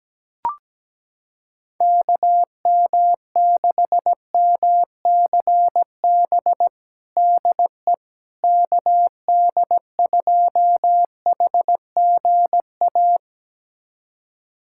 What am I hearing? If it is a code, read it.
Morse code, heard as a keyed tone, about 17 wpm, 700 Hz. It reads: KM6MCB DE KD2HGA
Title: USA Callsign DE Callsign 17wpm